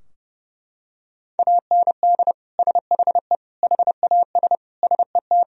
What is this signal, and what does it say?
Morse code, heard as a keyed tone, about 30 wpm, 700 Hz. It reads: AND SHE HAS SET
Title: Sentences from Top 1000 Words - Review Part 4 - 30wpm